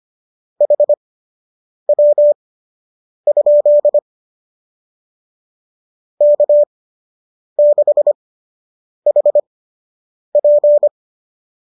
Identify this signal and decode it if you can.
Morse code, heard as a keyed tone, about 25 wpm, spaced as for 8 wpm, 590 Hz. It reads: HW? K6HP
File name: qso003 08